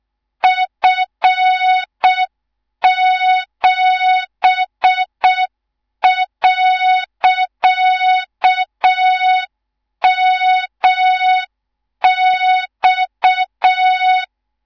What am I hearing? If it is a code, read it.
Morse code, heard as a keyed tone, about 6 wpm, 3700 Hz, with noise. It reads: F7.MX
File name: Morze-HA2MN / hang / 18 IRASJELEK2